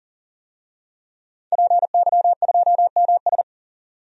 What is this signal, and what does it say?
Morse code, heard as a keyed tone, about 38 wpm, 700 Hz. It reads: PY2MS
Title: DXCC-100 Call Signs - 5 Characters - 40wpm